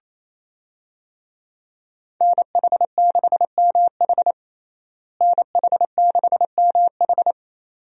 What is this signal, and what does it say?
Morse code, heard as a keyed tone, about 28 wpm, 700 Hz. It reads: NH6MH NH6MH
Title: USA Call Signs - 5 Characters - Repeated 2x - 28wpm